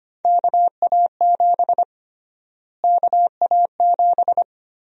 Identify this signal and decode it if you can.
Morse code, heard as a keyed tone, about 25 wpm, 700 Hz. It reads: KA7 KA7
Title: USA Call Sign Prefixes - Repeated 2x - 25wpm